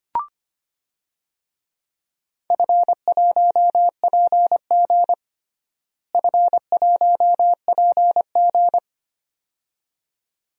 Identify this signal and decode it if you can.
Morse code, heard as a keyed tone, about 25 wpm, 700 Hz. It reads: F1PG F1PG
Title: DXCC-100 Call Signs - 4 Characters - Repeated 2x - 25wpm